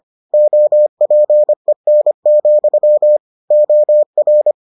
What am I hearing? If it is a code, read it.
Morse code, heard as a keyed tone, about 25 wpm, 600 Hz. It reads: OPEN, OR